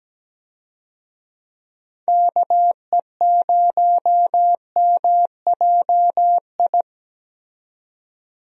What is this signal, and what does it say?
Morse code, heard as a keyed tone, about 17 wpm, 700 Hz. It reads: KE0MJI
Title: USA Call Signs - 6 Characters - 17wpm